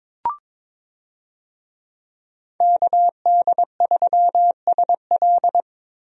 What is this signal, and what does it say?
Morse code, heard as a keyed tone, about 22 wpm, 700 Hz. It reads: KD3SL
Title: USA Call Signs - 5 Characters - 22wpm